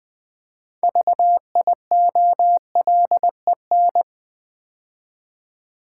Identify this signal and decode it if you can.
Morse code, heard as a keyed tone, about 20 wpm, 700 Hz. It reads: VIOLEN
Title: Build A Word - 3 Letters At A Time - 8-Letter Words - 20wpm